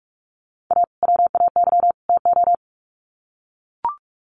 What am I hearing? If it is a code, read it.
Morse code, heard as a keyed tone, about 44 wpm, 700 Hz. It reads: A WAY TO
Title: Sets of 3 Words 45wpm